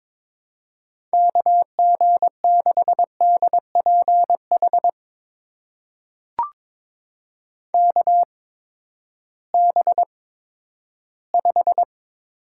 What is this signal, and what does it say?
Morse code, heard as a keyed tone, about 22 wpm, spaced as for 12 wpm, 700 Hz. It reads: KG6DPH KB5
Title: USA Call Signs - 6 Characters - 8x Character Spacing - 22wpm